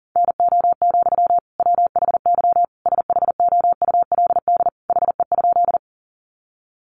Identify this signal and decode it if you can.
Morse code, heard as a keyed tone, about 40 wpm, 700 Hz. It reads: NO, WHY SHOULD HE?
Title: Sentences from Top 300 Words 40wpm